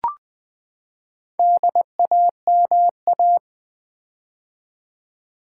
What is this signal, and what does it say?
Morse code, heard as a keyed tone, about 20 wpm, 700 Hz. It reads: DAMA
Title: Build A Word - 4 Letters At A Time - 8-Letter Words - 20wpm